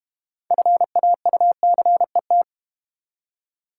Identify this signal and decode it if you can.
Morse code, heard as a keyed tone, about 32 wpm, 690 Hz. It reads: FAUCET